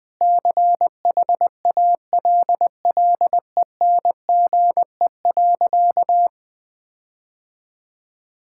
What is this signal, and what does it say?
Morse code, heard as a keyed tone, about 21 wpm, 700 Hz. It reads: CHALLENGE.
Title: Sentences with 4-Word Phrases - 20wpm